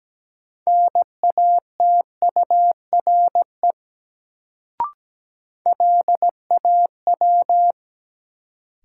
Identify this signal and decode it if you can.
Morse code, heard as a keyed tone, about 17 wpm, 700 Hz. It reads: NATURE LAW
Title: Sentences from Top 600 Words - New Word Drill - 17wpm